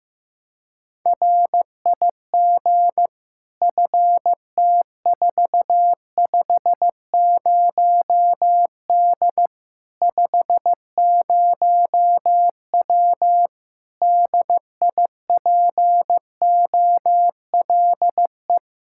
Morse code, heard as a keyed tone, about 15 wpm, 700 Hz. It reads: RIG FT450D 50W DIPOLE
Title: Practice QSO - Exchange 2 - Info - 15wpm